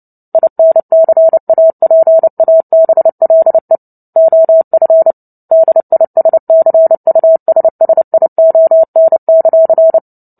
Morse code, heard as a keyed tone, about 30 wpm, 650 Hz. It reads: INCAPABLE OF DISCUSSION;
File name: Book Two / 03-Chapter 3 - 29 WPM